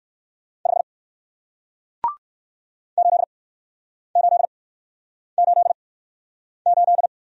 Morse code, heard as a keyed tone, about 56 wpm, spaced as for 9 wpm, 700 Hz. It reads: 5 8888
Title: Speed-Racing Single Letter-Number 45wpm